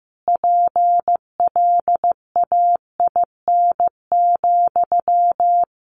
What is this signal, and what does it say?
Morse code, heard as a keyed tone, about 15 wpm, 700 Hz. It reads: PLAIN,
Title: Sentences from Top 600 Words 15wpm